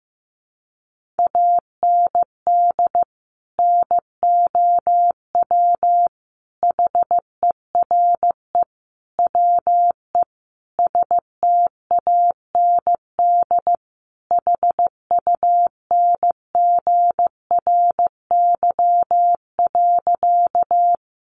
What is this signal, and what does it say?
Morse code, heard as a keyed tone, about 15 wpm, 700 Hz. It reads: AND NOW HERE WE STAND HUNGRY.
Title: Sentences from Top 1400 Words 15wpm